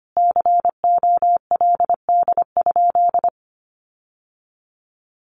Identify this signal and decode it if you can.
Morse code, heard as a keyed tone, about 25 wpm, 700 Hz. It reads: COLD?
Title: Sentences from Top 1400 Words 25wpm